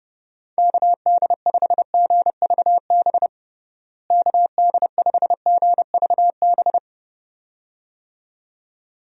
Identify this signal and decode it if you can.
Morse code, heard as a keyed tone, about 31 wpm, 700 Hz. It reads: KD5GVB KD5GVB
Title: USA Call Signs - 6 Characters - Repeated 2x - 30wpm